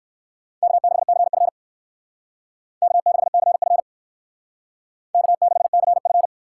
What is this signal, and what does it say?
Morse code, heard as a keyed tone, about 57 wpm, 700 Hz. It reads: K6CF K6CF K6CF